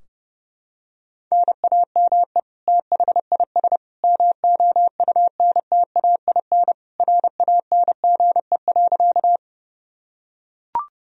Morse code, heard as a keyed tone, about 30 wpm, 700 Hz. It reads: NAME THIS MOUNTAIN RANGE.
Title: Sentences from Top 1000 Words - Review Part 4 - 30wpm